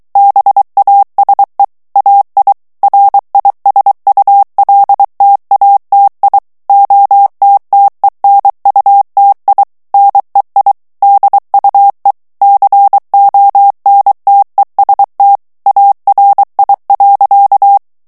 Morse code, heard as a keyed tone, about 23 wpm, 800 Hz. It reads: BASE AI RISULTATI OTTENUTI NEI DUE CONTEST ARI.
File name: ITALIEN texte clair 2 vitesse 22 mots minute